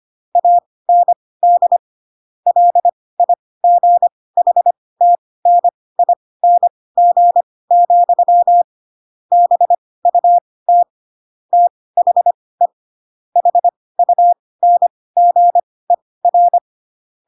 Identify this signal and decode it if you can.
Morse code, heard as a keyed tone, about 25 wpm, spaced as for 18 wpm, 700 Hz. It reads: AND LIGHTNING, BUT THE HUNGER
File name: Pinocchio / Pinocchio ch6 12 25 ik1hge